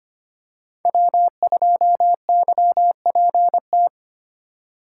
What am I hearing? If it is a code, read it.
Morse code, heard as a keyed tone, about 25 wpm, 700 Hz. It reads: W2YPT
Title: USA Call Signs - 5 Characters - 8x Character Spacing - 25wpm